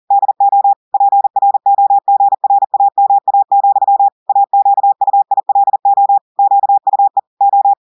Morse code, heard as a keyed tone, about 40 wpm, 800 Hz. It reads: DO PROGRAMA, AQUILO QUE O
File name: audios / portugais / portugais texte en clair 2 vitesse 40 mots minutes